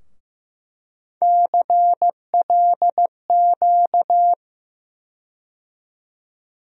Morse code, heard as a keyed tone, about 15 wpm, 700 Hz. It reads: CLQ
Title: USA Call Sign Postfixes - 3 Letters 15wpm